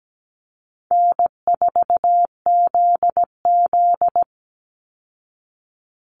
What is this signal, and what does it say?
Morse code, heard as a keyed tone, about 17 wpm, 700 Hz. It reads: N4ZZ
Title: Call Signs Repeated 3x - 17wpm